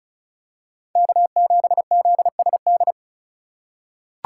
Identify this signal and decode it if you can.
Morse code, heard as a keyed tone, about 34 wpm, 700 Hz. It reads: K7ZSD
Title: USA Call Signs - 5 Characters - 3x Character Spacing - 35wpm